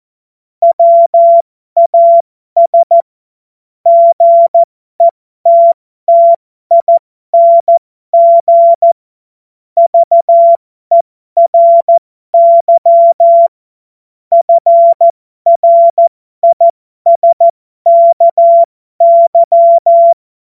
Morse code, heard as a keyed tone, about 14 wpm, spaced as for 12 wpm, 680 Hz. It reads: WAS GETTING VERY FRISKY
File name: Morsekurs / Witze